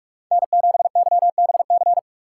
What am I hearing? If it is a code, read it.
Morse code, heard as a keyed tone, about 42 wpm, 700 Hz. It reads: N7YBC
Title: USA Call Signs - 5 Characters - 6x Character Spacing - 45wpm